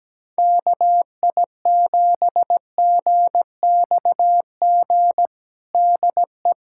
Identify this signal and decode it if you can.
Morse code, heard as a keyed tone, about 17 wpm, 700 Hz. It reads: KI7GXG DE
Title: USA Callsign DE Callsign 17wpm